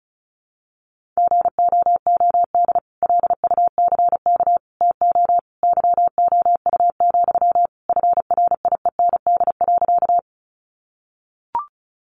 Morse code, heard as a keyed tone, about 34 wpm, 700 Hz. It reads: GOOD LUCK TO YOU, FRIEND.
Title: Sentences from Top 1500 Words 35wpm